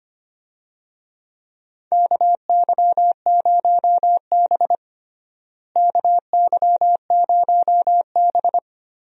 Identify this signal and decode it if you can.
Morse code, heard as a keyed tone, about 25 wpm, 700 Hz. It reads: KY0B KY0B
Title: USA Call Signs - 4 Characters - Repeated 3x - 25wpm